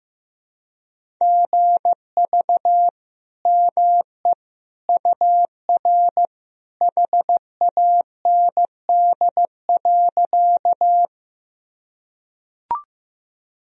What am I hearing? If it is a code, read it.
Morse code, heard as a keyed tone, about 15 wpm, 700 Hz. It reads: GV ME UR HAND.
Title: Sentences from Top 200 Words and Abbreviations 15wpm